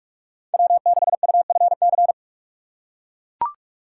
Morse code, heard as a keyed tone, about 43 wpm, 700 Hz. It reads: W6UFC